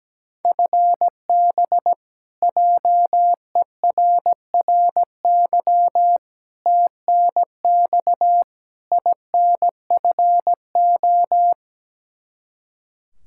Morse code, heard as a keyed tone, about 17 wpm, 700 Hz. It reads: FB JERRY TNX INFO
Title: K1USN SST QSO Practice 17wpm